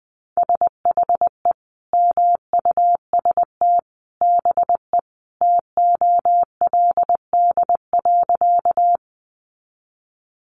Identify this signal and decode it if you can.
Morse code, heard as a keyed tone, about 20 wpm, 700 Hz. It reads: SHE MUST BE TOLD.
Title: Sentences from Top 400 Words 20wpm